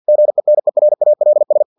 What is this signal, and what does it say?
Morse code, heard as a keyed tone, about 50 wpm, 600 Hz. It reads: GENERALS
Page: Never 49wpm